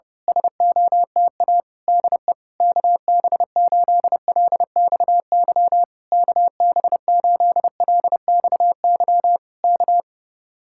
Morse code, heard as a keyed tone, about 25 wpm, 700 Hz. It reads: SOTA DE KB8LXY KB8LXY K
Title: Speed-Racing Common QSO Phrases Encore - 25wpm